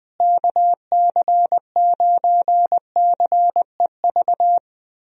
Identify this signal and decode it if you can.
Morse code, heard as a keyed tone, about 21 wpm, 700 Hz. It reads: KC9CEV
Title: USA Call Signs - 6 Characters - Repeated 2x - 20wpm